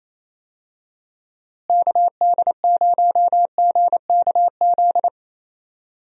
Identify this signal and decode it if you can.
Morse code, heard as a keyed tone, about 28 wpm, 700 Hz. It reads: KD0GKZ
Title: USA Call Signs - 6 Characters - Repeated 3x - 28wpm